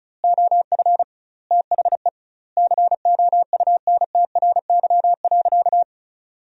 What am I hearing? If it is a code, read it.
Morse code, heard as a keyed tone, about 36 wpm, 700 Hz. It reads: OF THE COUNTRY.
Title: 2x Word-Spacing - Sentences from Top 300 Words - 35wpm